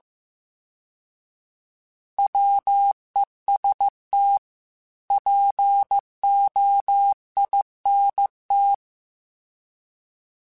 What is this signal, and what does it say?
Morse code, heard as a keyed tone, about 15 wpm, 800 Hz. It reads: WEST POINT